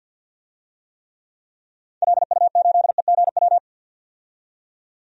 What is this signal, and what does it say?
Morse code, heard as a keyed tone, about 49 wpm, 700 Hz. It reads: LU8EGW
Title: DXCC-100 Call Signs - 6 Characters - 50wpm